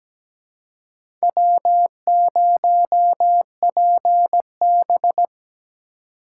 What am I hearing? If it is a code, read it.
Morse code, heard as a keyed tone, about 17 wpm, 700 Hz. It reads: W0PB
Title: USA Call Signs - 2x Character Spacing - 17wpm